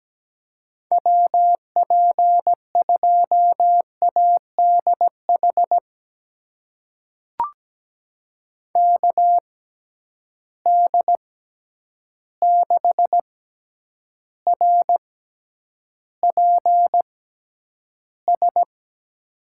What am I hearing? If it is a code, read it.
Morse code, heard as a keyed tone, about 17 wpm, spaced as for 9 wpm, 700 Hz. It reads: WP2ADH KD6RPS
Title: USA Call Signs - 6 Characters - 6x Character Spacing - 17wpm